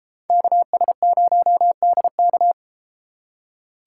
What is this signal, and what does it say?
Morse code, heard as a keyed tone, about 33 wpm, 700 Hz. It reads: KS0DK